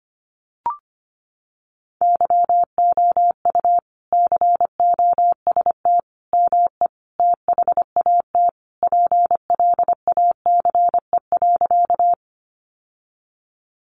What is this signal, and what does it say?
Morse code, heard as a keyed tone, about 25 wpm, 700 Hz. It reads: YOU COST ME THAT PLACE.
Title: Sentences from Top 800 Words 25wpm